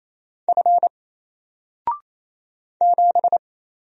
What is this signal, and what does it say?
Morse code, heard as a keyed tone, about 27 wpm, 700 Hz. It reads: F 7
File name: Single-Letter-Number-28wpm